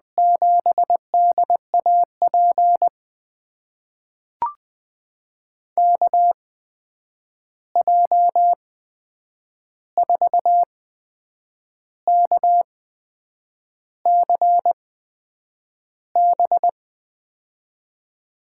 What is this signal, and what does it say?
Morse code, heard as a keyed tone, about 20 wpm, spaced as for 8 wpm, 700 Hz. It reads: 7DAP KJ4KCB
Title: USA Call Signs - 6 Characters - 8x Character Spacing - 20wpm